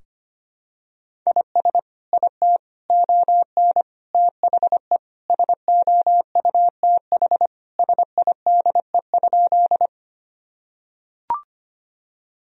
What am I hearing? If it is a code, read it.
Morse code, heard as a keyed tone, about 25 wpm, 700 Hz. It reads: IS IT ON THE SOUTH SIDE?
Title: Sentences from Top 1000 Words - Review Part 2 - 25wpm